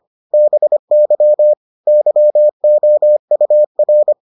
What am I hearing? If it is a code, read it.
Morse code, heard as a keyed tone, about 25 wpm, 600 Hz. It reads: BY YOUR